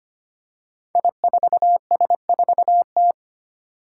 Morse code, heard as a keyed tone, about 25 wpm, 700 Hz. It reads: I4S4T
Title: Five Challenging Letter-Number Combinations 25wpm